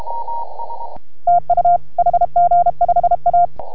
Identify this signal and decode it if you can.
Morse code, heard as a keyed tone, about 33 wpm, 690 Hz, with radio noise. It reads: TU HG5A